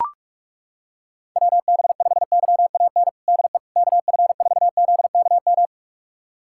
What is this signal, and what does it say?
Morse code, heard as a keyed tone, about 43 wpm, 700 Hz. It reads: WB5YAN DE KF4ZKM
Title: USA Callsign DE Callsign 45wpm